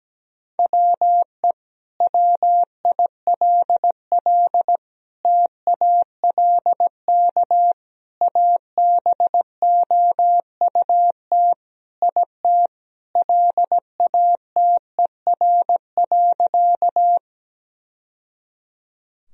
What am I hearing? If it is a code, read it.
Morse code, heard as a keyed tone, about 17 wpm, 700 Hz. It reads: WE WILL TALK ABOUT IT LATER.
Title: Sentences from Top 1000 Words - Review Part 4 - 17wpm